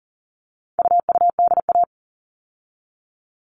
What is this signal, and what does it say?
Morse code, heard as a keyed tone, about 40 wpm, 700 Hz. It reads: UUDA
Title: All Cut Numbers - 4-Digit 40wpm